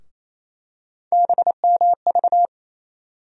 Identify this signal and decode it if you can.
Morse code, heard as a keyed tone, about 28 wpm, 700 Hz. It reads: BMV
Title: USA Call Sign Postfixes - 3 Letters 28wpm